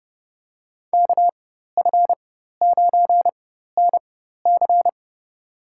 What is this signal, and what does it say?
Morse code, heard as a keyed tone, about 31 wpm, spaced as for 14 wpm, 700 Hz. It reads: KF9NC
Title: USA Call Signs - 5 Characters - 4x Character Spacing - 30wpm